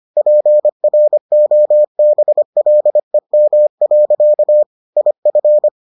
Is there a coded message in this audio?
Yes: Morse code from 0.2 to 5.7 s, heard as a keyed tone, about 25 wpm, 600 Hz. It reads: PROBLEM. IF